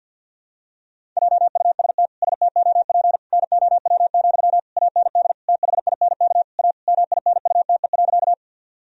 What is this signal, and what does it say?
Morse code, heard as a keyed tone, about 52 wpm, 700 Hz. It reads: JUST STOP NOW, AND THINK A MINUTE.
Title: Sentences from Top 1000 Words - Review Part 1 - 50wpm